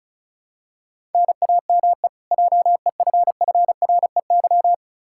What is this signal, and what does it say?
Morse code, heard as a keyed tone, about 36 wpm, 700 Hz. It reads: NAME JEFFREY
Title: Practice QSO - Exchange 1 - Beginning - 35wpm